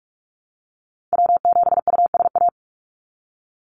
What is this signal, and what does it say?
Morse code, heard as a keyed tone, about 44 wpm, 700 Hz. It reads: W7USA
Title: NA SOTA Activator Callsigns 45wpm